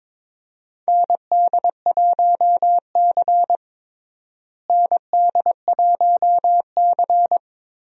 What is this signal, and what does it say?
Morse code, heard as a keyed tone, about 22 wpm, 700 Hz. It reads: ND1C ND1C
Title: USA Call Signs - 4 Characters - Repeated 3x - 22wpm